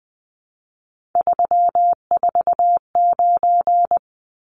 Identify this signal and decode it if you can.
Morse code, heard as a keyed tone, about 20 wpm, 700 Hz. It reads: 349
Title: Numbers - 3-Digit 20wpm